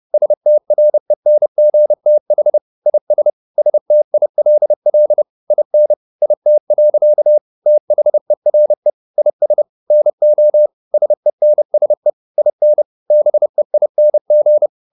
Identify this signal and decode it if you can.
Morse code, heard as a keyed tone, about 30 wpm, 600 Hz. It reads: STRENGTH IS STILL IN IT. THERE IS NO SENSE IN BEING
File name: the old man and the sea / the old man and the sea 1 30wpm